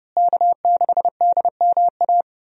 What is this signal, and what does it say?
Morse code, heard as a keyed tone, about 29 wpm, 700 Hz. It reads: K6DMA
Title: USA Call Signs - 5 Characters - Repeated 2x - 30wpm